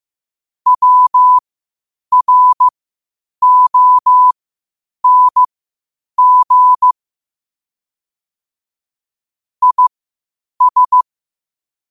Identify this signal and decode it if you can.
Morse code, heard as a keyed tone, about 15 wpm, spaced as for 8 wpm, 1000 Hz. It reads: WRONG IS